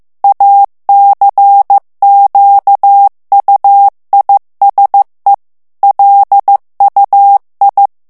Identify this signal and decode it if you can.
Morse code, heard as a keyed tone, about 15 wpm, 800 Hz. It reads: ACQUISE LUI
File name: morse mp3 / FRANCAIS TEXTE CLAIR 2 VITESSE 14 MOTS MINUTES